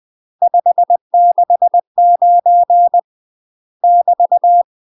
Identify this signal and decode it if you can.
Morse code, heard as a keyed tone, about 20 wpm, 700 Hz. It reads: 569 <BT>
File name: UK MASTER QSOs TRAINING / 20WPM / UK MASTER QSOs 0048